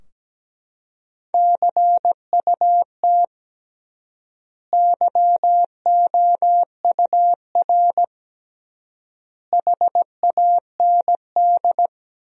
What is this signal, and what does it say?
Morse code, heard as a keyed tone, about 17 wpm, 700 Hz. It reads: CUT YOUR HAND
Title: 3x Word-Spacing - Sentences from Top 300 Words - 17wpm